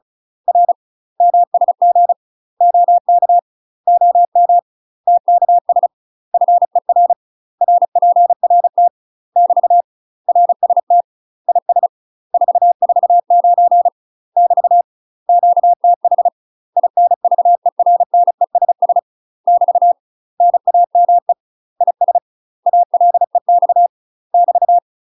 Morse code, heard as a keyed tone, about 34 wpm, 700 Hz. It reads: R MSG OK OM TKS FER RPRT <BT> RST IS 449 <BT> QTH INVERNESS <BT> NAME IS ALEX <BT>
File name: UK MASTER QSOs TRAINING / 35WPM / UK MASTER QSOs 0039